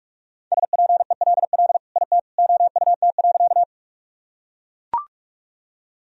Morse code, heard as a keyed tone, about 41 wpm, 700 Hz. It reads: SPELL IT OUT.
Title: Sentences from Top 1000 Words - Review Part 1 - 45wpm